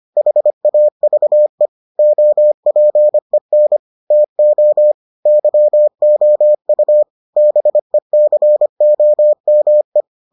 Morse code, heard as a keyed tone, about 25 wpm, 600 Hz. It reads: HAVE OPEN TO YOU BECOME